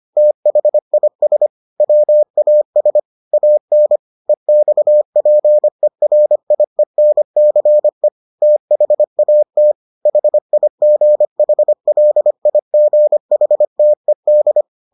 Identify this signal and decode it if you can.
Morse code, heard as a keyed tone, about 25 wpm, 600 Hz. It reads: THIS WAS AN EXPERIENCE THAT HIGHLIGHTED